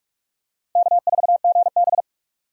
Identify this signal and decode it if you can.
Morse code, heard as a keyed tone, about 45 wpm, 700 Hz. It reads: K4GB
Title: USA Call Signs - 4 Characters - 45wpm